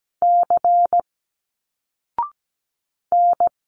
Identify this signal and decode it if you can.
Morse code, heard as a keyed tone, about 17 wpm, 700 Hz. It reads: C N
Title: Single Letters 17wpm